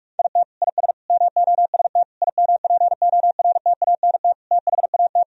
Morse code, heard as a keyed tone, about 46 wpm, 700 Hz. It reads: IT IS MOST IMPORTANT THAT
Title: Sentences from Top 1000 Words - Review Part 1 - 45wpm